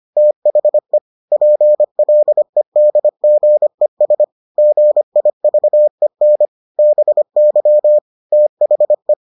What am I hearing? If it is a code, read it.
Morse code, heard as a keyed tone, about 25 wpm, 600 Hz. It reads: THE PLEDGES GIVEN BY THE